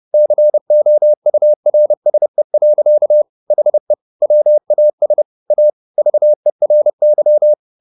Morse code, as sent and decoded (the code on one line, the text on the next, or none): -.-. --- ..- .-. ... .-.-.- / .... / .-- .- ... / .- / ...- . .-. -.--
COURSE. HE WAS A VERY